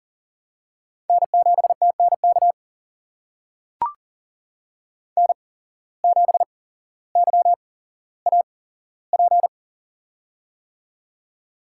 Morse code, heard as a keyed tone, about 40 wpm, spaced as for 17 wpm, 700 Hz. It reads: N7TNK N7YAP
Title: USA Call Signs - 5 Characters - 8x Character Spacing - 40wpm